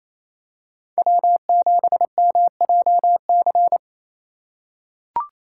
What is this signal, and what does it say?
Morse code, heard as a keyed tone, about 28 wpm, 700 Hz. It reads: W7MJC
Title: USA Call Signs - 2x Character Spacing - 28wpm